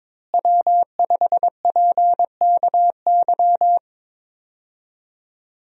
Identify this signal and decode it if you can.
Morse code, heard as a keyed tone, about 22 wpm, 700 Hz. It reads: W5PKY
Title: USA Call Signs - 5 Characters - Repeated 3x - 22wpm